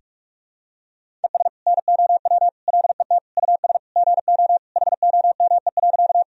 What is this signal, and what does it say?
Morse code, heard as a keyed tone, about 46 wpm, 700 Hz. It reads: ES NOW LET US GO HOME.
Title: Sentences from Top 300 Words and Abbreviations 45wpm